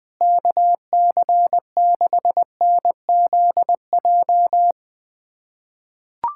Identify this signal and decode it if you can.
Morse code, heard as a keyed tone, about 20 wpm, 700 Hz. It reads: KC6NZJ
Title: USA Callsign DE Callsign 20wpm